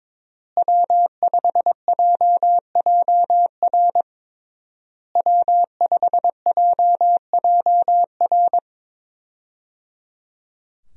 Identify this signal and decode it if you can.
Morse code, heard as a keyed tone, about 22 wpm, 700 Hz. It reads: W5JJR W5JJR
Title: USA Call Signs - 5 Characters - Repeated 3x - 22wpm